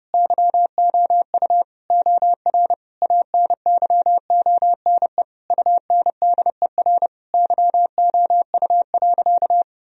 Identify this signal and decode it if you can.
Morse code, heard as a keyed tone, about 30 wpm, 700 Hz. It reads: YOU OR ANYONE UNDER YOU.
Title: Sentences from Top 1000 Words - Review Part 1 - 30wpm